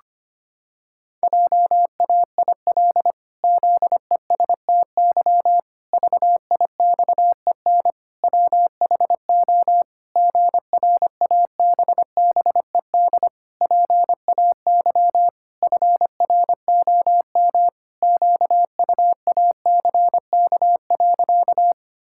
Morse code, heard as a keyed tone, about 25 wpm, 700 Hz. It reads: JAIL ZESTY VIXEN WHO GRABBED PAY FROM QUACK.
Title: Pangram Challenge v2 - 25wpm